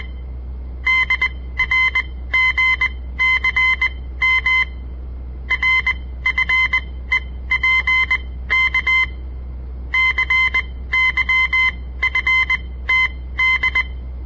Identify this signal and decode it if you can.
Morse code, heard as a keyed tone, about 20 wpm, spaced as for 15 wpm, 2050 Hz. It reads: DRGCM RFEPK CYFTD
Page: On the air Morse code practice from the Southern Montana Amatuer Radio Association; all code, no read back